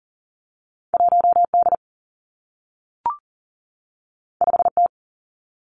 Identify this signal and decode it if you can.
Morse code, heard as a keyed tone, about 40 wpm, 700 Hz. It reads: 1 D 5 T
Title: Two Challenging Letter-Number Combinations 40wpm